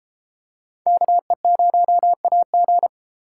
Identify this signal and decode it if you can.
Morse code, heard as a keyed tone, about 31 wpm, 700 Hz. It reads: KE0AG